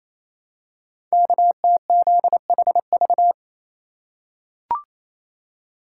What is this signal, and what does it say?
Morse code, heard as a keyed tone, about 29 wpm, 700 Hz. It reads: KTZHV
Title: Code Groups - 5 Letters - 28wpm